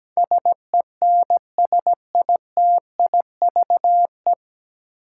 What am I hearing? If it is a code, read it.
Morse code, heard as a keyed tone, about 17 wpm, 700 Hz. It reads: SENSITIVE
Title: Sentences from Top 1400 Words - New Word Drill Alt - 17wpm